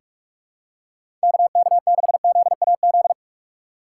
Morse code, heard as a keyed tone, about 47 wpm, 700 Hz. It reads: KK6ZAZ
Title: USA Call Signs - 3x Character Spacing - 45wpm